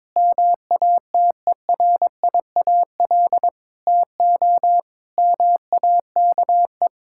Morse code, heard as a keyed tone, about 22 wpm, 700 Hz. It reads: MATERIAL TO MAKE
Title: Sentences from Top 1000 Words - Review Part 5 - 22wpm